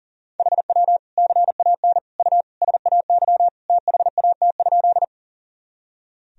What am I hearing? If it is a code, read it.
Morse code, heard as a keyed tone, about 40 wpm, 700 Hz. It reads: HW CAN U SAY THAT?
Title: Sentences from Top 200 Words and Abbreviations 40wpm